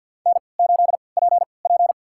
Morse code, heard as a keyed tone, about 47 wpm, spaced as for 28 wpm, 700 Hz. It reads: N8PP
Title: USA Call Signs - 3x Character Spacing - 50wpm